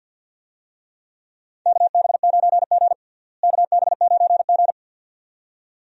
Morse code, heard as a keyed tone, about 52 wpm, 700 Hz. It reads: KB9G KB9G
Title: USA Call Signs - 4 Characters - Repeated 2x - 50wpm